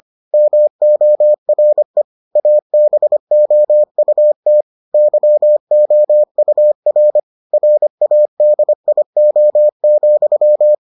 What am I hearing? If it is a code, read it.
Morse code, heard as a keyed tone, about 25 wpm, 600 Hz. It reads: MORE ABOUT YOUR RADIO,